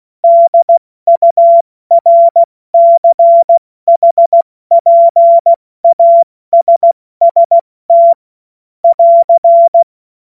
Morse code, heard as a keyed tone, about 16 wpm, 680 Hz. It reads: DURCHPASST <AR>
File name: Morsekurs / Witze / Witz 27. Juni 2023 70 F80